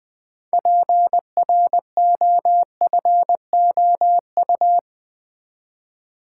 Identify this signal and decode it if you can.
Morse code, heard as a keyed tone, about 20 wpm, 700 Hz. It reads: PROFOU
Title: Build A Word - 3 Letters At A Time - 8-Letter Words - 20wpm